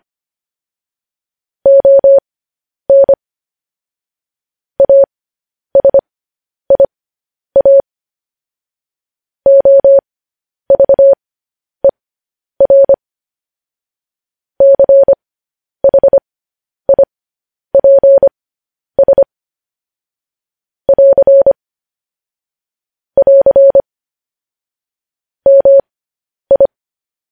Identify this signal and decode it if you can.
Morse code, heard as a keyed tone, about 25 wpm, spaced as for 10 wpm, 550 Hz. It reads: ON ASIA OVER CHIPS <AR> <AR> MI
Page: news-headlines-in-morse-code-at-10-wpm